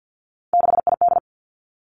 Morse code, heard as a keyed tone, about 50 wpm, 700 Hz. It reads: 6ID
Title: Three Challenging Letter-Number Combinations 50wpm